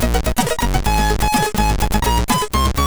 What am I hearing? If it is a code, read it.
There is no code.